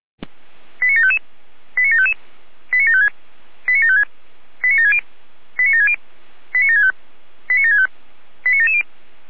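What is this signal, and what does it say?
Five-tone selective call (ZVEI), heard as quick runs of five tones, tones of about 70 ms. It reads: 89755 89755 89756 89756 89760 89760 89765 89765 89770
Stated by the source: ZVEI five-tone selective call 89720, received via SDR.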